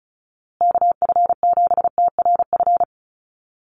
Morse code, heard as a keyed tone, about 34 wpm, 700 Hz. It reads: KF7TRF
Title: USA Call Signs Repeated 2x 35wpm